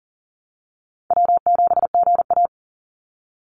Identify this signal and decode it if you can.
Morse code, heard as a keyed tone, about 38 wpm, 700 Hz. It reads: W7GA
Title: NA SOTA Chaser Callsigns 40wpm